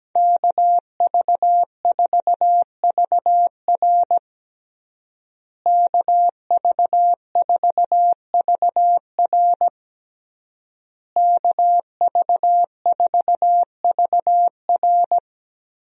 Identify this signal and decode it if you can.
Morse code, heard as a keyed tone, about 17 wpm, 700 Hz. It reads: KV4VR KV4VR KV4VR
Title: USA Call Signs - 5 Characters - Repeated 3x - 17wpm